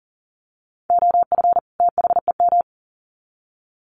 Morse code, heard as a keyed tone, about 40 wpm, 700 Hz. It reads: OF THEM
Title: Sets of 2 Words 40wpm